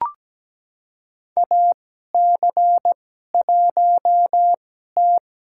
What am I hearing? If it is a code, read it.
Morse code, heard as a keyed tone, about 17 wpm, spaced as for 12 wpm, 700 Hz. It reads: AC1T
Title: USA Call Signs - 4 Characters - 2x Character Spacing - 17wpm